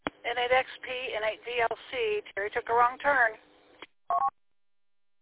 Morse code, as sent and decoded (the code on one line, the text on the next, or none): none